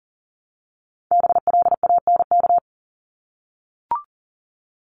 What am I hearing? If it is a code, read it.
Morse code, heard as a keyed tone, about 37 wpm, 700 Hz. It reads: BLANK
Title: Top 1251-1500 Words 40wpm